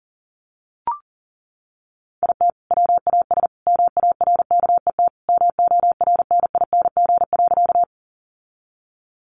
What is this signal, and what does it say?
Morse code, heard as a keyed tone, about 40 wpm, 700 Hz. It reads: IT WAS MARKET MORNING.
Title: Sentences from Top 600 Words 40wpm